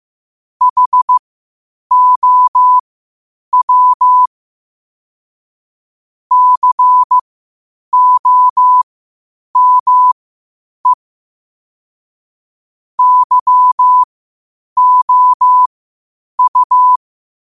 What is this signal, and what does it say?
Morse code, heard as a keyed tone, about 15 wpm, spaced as for 8 wpm, 1000 Hz. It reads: HOW COME YOU
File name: qotdfiles / 8wpm